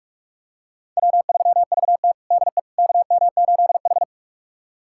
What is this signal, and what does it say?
Morse code, heard as a keyed tone, about 41 wpm, 700 Hz. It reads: W3VT DE KM8H
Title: USA Callsign DE Callsign - 4 Characters - 45wpm